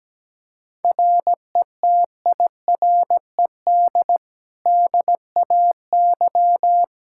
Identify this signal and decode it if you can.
Morse code, heard as a keyed tone, about 17 wpm, 700 Hz. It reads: RETIRED DAY
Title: Practice QSO - Exchange 3 - More Info - 17wpm